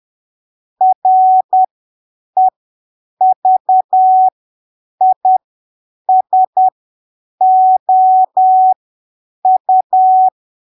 Morse code, heard as a keyed tone, about 10 wpm, spaced as for 7 wpm, 750 Hz. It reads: REVISOU
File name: arquivos / texto / texto 07 10ppm